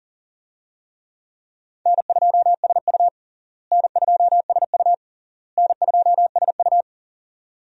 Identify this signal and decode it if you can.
Morse code, heard as a keyed tone, about 37 wpm, 700 Hz. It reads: N2SU N2SU N2SU
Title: USA Call Signs - 4 Characters - Repeated 3x - 40wpm